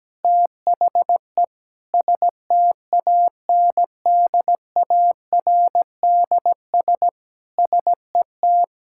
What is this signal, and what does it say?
Morse code, heard as a keyed tone, about 17 wpm, 700 Hz. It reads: THE STANDARDS SET
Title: Sentences from Top 1000 Words - Review Part 6 - 17wpm